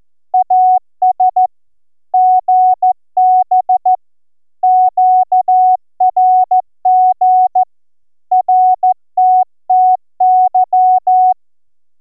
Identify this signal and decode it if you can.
Morse code, heard as a keyed tone, about 13 wpm, 740 Hz. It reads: AS GB QRG RTTY